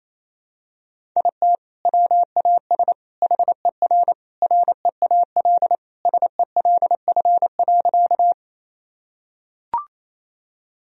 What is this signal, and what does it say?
Morse code, heard as a keyed tone, about 28 wpm, 700 Hz. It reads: IT WAS HER REAL SELF.
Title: Sentences from Top 1000 Words - Review Part 1 - 28wpm